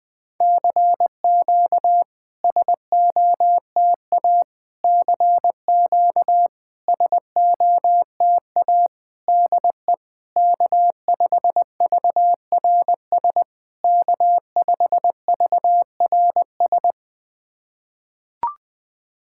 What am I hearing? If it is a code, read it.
Morse code, heard as a keyed tone, about 20 wpm, 700 Hz. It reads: CQ SOTA CQ SOTA DE K5VRS K5VRS
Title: SOTA QSO Practice 20wpm